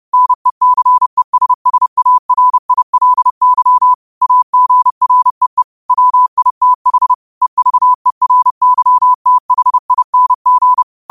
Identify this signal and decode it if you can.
Morse code, heard as a keyed tone, about 30 wpm, 1000 Hz. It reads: NECESSARILY AGREE WITH EVERYTHING
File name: qotdfiles / 30wpm